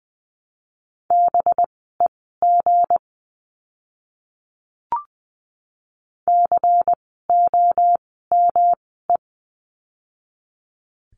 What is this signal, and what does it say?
Morse code, heard as a keyed tone, about 20 wpm, spaced as for 14 wpm, 700 Hz. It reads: BEG COME
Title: 185 Lesson 30f - Letter b - Words 20wpm